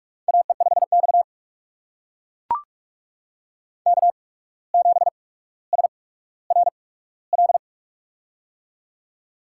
Morse code, heard as a keyed tone, about 44 wpm, spaced as for 18 wpm, 700 Hz. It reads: AE5X K7SRL
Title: USA Call Signs - 8x Character Spacing - 45wpm